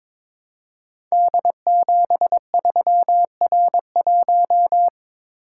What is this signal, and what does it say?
Morse code, heard as a keyed tone, about 23 wpm, 700 Hz. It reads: D73R1
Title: Code Groups - 5 Letter:Numbers - 22wpm